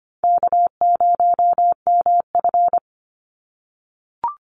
Morse code, heard as a keyed tone, about 25 wpm, 700 Hz. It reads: K0MF
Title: USA Call Signs 25wpm